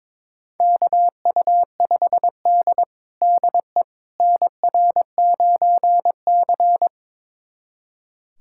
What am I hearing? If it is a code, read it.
Morse code, heard as a keyed tone, about 22 wpm, 700 Hz. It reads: KU5D DE NR9C
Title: USA Callsign DE Callsign - 4 Characters - 22wpm